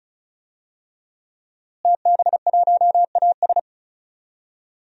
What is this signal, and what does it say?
Morse code, heard as a keyed tone, about 36 wpm, 700 Hz. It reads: TB1AS
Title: DXCC-100 Call Signs - 5 Characters - Repeated 2x - 35wpm